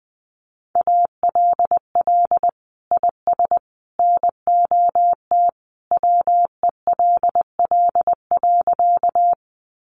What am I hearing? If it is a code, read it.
Morse code, heard as a keyed tone, about 20 wpm, 700 Hz. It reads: ALL IS NOT WELL.
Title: Sentences from Top 600 Words 20wpm